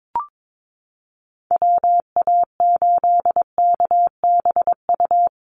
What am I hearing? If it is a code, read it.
Morse code, heard as a keyed tone, about 22 wpm, 700 Hz. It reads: WA8KBU
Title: Call Signs Repeated 2x - 22wpm